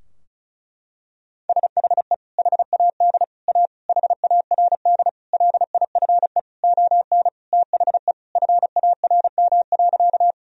Morse code, heard as a keyed tone, about 34 wpm, 700 Hz. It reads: SHE HAD A HARD LIFE ON THE FARM.
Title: Sentences from Top 1000 Words - Review Part 2 - 35wpm